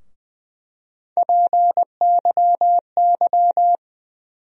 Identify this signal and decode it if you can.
Morse code, heard as a keyed tone, about 20 wpm, 700 Hz. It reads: PYY
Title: USA Call Sign Postfixes - 3 Letters 20wpm